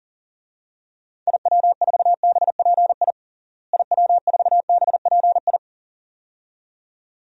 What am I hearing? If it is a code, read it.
Morse code, heard as a keyed tone, about 39 wpm, 700 Hz. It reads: IW4BPI IW4BPI
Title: DXCC-100 Call Signs - 6 Characters - Repeated 2x - 40wpm